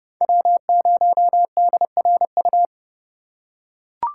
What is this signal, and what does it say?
Morse code, heard as a keyed tone, about 30 wpm, 700 Hz. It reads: W0DRU
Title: USA Call Signs - 5 Characters - 3x Character Spacing - 30wpm